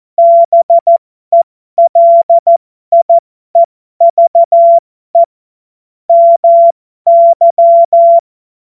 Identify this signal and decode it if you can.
Morse code, heard as a keyed tone, about 14 wpm, spaced as for 12 wpm, 680 Hz. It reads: BELIEVE MY